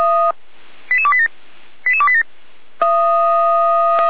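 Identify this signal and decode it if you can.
Five-tone selective call (ZVEI), heard as quick runs of five tones, tones of about 70 ms. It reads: 88287 88287
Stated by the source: ZVEI five-tone call received by SDR, recorded by Feuerwehr Orlamünde (f85360).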